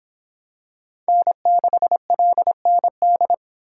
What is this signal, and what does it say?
Morse code, heard as a keyed tone, about 23 wpm, 700 Hz. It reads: N6LND